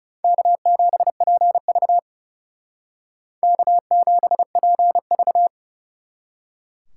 Morse code, heard as a keyed tone, about 40 wpm, 700 Hz. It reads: K7PV K7PV